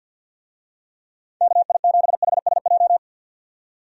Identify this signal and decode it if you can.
Morse code, heard as a keyed tone, about 52 wpm, 700 Hz. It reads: KI7HSJ
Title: USA Call Signs - 6 Characters - 50wpm